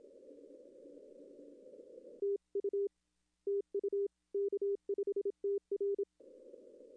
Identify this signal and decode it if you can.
Morse code, heard as a keyed tone, about 27 wpm, 400 Hz. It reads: TU TU K5TR
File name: contests / arrl-160-k5tr-op-w5tn -0200